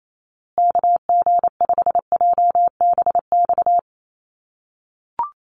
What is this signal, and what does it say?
Morse code, heard as a keyed tone, about 28 wpm, 700 Hz. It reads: KG5JBX
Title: USA Call Signs Repeated 3x 28wpm